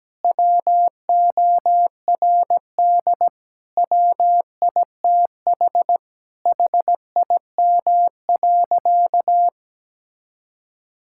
Morse code, heard as a keyed tone, about 17 wpm, 700 Hz. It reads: WORD WITH HIM.
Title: Sentences from Top 1000 Words - Review Part 6 - 17wpm